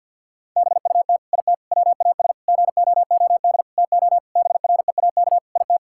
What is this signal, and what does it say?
Morse code, heard as a keyed tone, about 50 wpm, 700 Hz. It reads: BUT IT WAS GOOD TO BREAK IT
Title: Sentences from Top 1000 Words - Review Part 6 - 50wpm